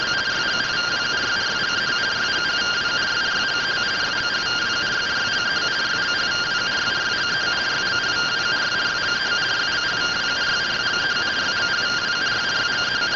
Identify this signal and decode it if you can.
Two-tone teleprinter signal, about 170 Hz shift.